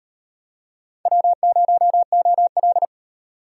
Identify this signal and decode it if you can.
Morse code, heard as a keyed tone, about 32 wpm, 700 Hz. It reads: W0OL